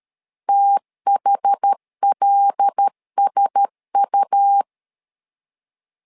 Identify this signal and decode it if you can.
Morse code, heard as a keyed tone, about 13 wpm, 800 Hz. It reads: THLSU
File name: LEZIONI / 5 LBVC / 5 L 12wpm 1913